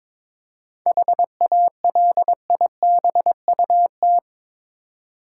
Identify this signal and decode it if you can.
Morse code, heard as a keyed tone, about 23 wpm, 700 Hz. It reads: HALIBUT
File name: practice / 22wpm / Difficult Words with Hints - 22wpm